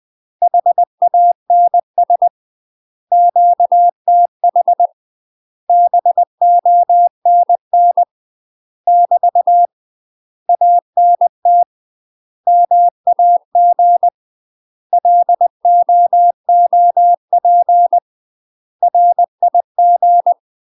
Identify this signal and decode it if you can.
Morse code, heard as a keyed tone, about 20 wpm, 700 Hz. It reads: HANS QTH BONN <BT> ANT MAG LOOP RIG
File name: UK MASTER QSOs TRAINING / 20WPM / UK MASTER QSOs 0002